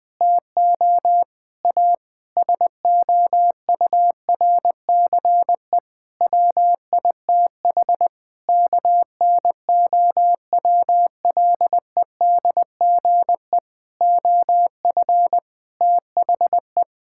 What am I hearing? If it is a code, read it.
Morse code, heard as a keyed tone, about 21 wpm, 700 Hz. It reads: TO A SOURCE WITH KNOWLEDGE OF THE
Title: Sets of 9 Words - 20wpm